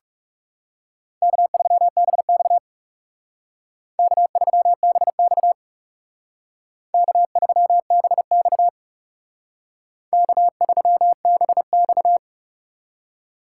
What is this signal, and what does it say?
Morse code, heard as a keyed tone, about 38 wpm, 700 Hz. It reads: K3BX K3BX K3BX K3BX